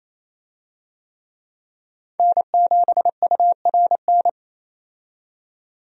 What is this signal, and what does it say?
Morse code, heard as a keyed tone, about 28 wpm, 700 Hz. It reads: N7URN
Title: USA Call Signs - 5 Characters - 28wpm